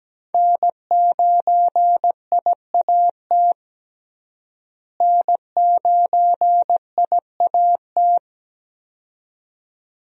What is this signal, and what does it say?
Morse code, heard as a keyed tone, about 17 wpm, 700 Hz. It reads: N9IAT N9IAT
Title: USA Call Signs - 5 Characters - Repeated 2x - 17wpm